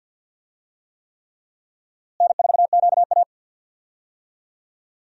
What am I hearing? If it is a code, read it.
Morse code, heard as a keyed tone, about 50 wpm, 700 Hz. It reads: N4QA
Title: USA Call Signs - 4 Characters - Repeated 3x - 50wpm